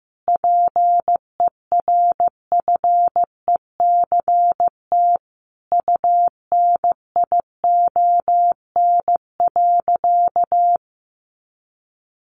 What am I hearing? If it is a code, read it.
Morse code, heard as a keyed tone, about 14 wpm, 700 Hz. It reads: PERFECT UNION.
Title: Sentences from Top 1400 Words 15wpm